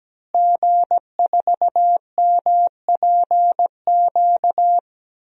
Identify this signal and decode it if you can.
Morse code, heard as a keyed tone, about 17 wpm, 700 Hz. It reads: G4MPQ
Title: DXCC-100 Call Signs - 5 Characters - Repeated 3x - 17wpm